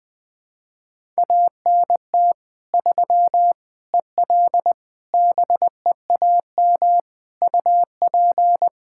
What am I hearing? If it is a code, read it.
Morse code, heard as a keyed tone, about 21 wpm, 700 Hz. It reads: ANT 3 EL BEAM UP
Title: K1USN SST QSO Practice 20wpm